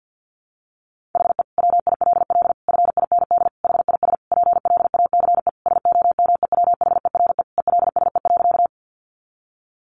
Morse code, heard as a keyed tone, about 50 wpm, 700 Hz. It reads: HE WILL FIND HIS PLACE SOMEWHERE ELSE.
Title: Sentences from Top 1400 Words 50wpm